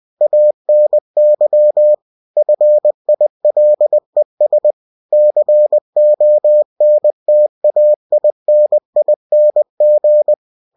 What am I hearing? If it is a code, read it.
Morse code, heard as a keyed tone, about 20 wpm, 600 Hz. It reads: ANY FILES CONTAINING